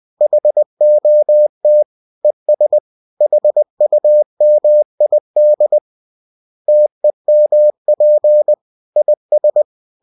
Morse code, heard as a keyed tone, about 20 wpm, 600 Hz. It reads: HOT ES HUMID TEMP IS